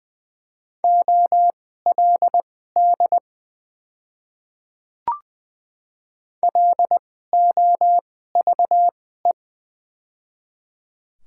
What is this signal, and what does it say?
Morse code, heard as a keyed tone, about 20 wpm, spaced as for 14 wpm, 700 Hz. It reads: OLD LOVE
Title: 2x Character Spacing - Top 500 Words - 20wpm